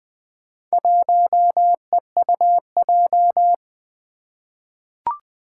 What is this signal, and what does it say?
Morse code, heard as a keyed tone, about 20 wpm, 700 Hz. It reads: 1EUJ
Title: Four Challenging Letter-Number Combinations 20wpm